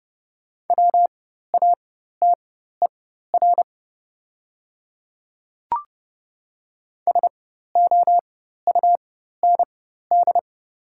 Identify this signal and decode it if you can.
Morse code, heard as a keyed tone, about 30 wpm, spaced as for 14 wpm, 700 Hz. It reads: WATER SOUND
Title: 4x Character Spacing - Top 200 Words - 30wpm